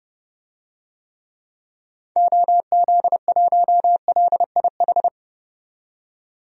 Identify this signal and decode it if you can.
Morse code, heard as a keyed tone, about 31 wpm, 700 Hz. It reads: OZ1LIH
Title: DXCC-100 Call Signs - 6 Characters - 30wpm